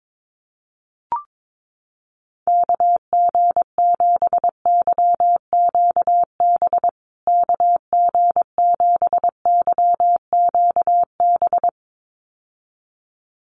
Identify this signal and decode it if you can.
Morse code, heard as a keyed tone, about 22 wpm, 700 Hz. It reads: KG7YQB KG7YQB
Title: USA Call Signs Repeated 2x 22wpm